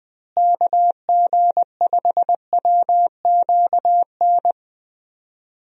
Morse code, heard as a keyed tone, about 21 wpm, 700 Hz. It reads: KG5WQN
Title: USA Call Signs - 6 Characters - Repeated 3x - 20wpm